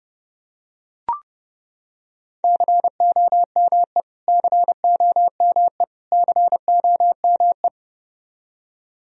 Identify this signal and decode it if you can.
Morse code, heard as a keyed tone, about 31 wpm, 700 Hz. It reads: COME COME COME
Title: Top 100 Words Repeated 30wpm